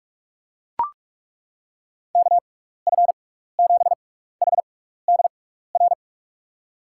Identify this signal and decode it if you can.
Morse code, heard as a keyed tone, about 43 wpm, spaced as for 15 wpm, 700 Hz. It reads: KF7HDR
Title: USA Call Signs - 6 Characters - 6x Character Spacing - 45wpm